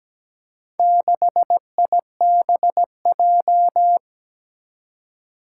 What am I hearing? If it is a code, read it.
Morse code, heard as a keyed tone, about 17 wpm, 700 Hz. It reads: 6IBJ
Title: Four Challenging Letter-Number Combinations 17wpm